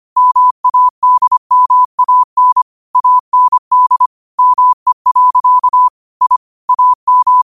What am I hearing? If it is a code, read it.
Morse code, heard as a keyed tone, about 25 wpm, 1000 Hz. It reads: MADMAN AND ME. I AM
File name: qotdfiles / 25wpm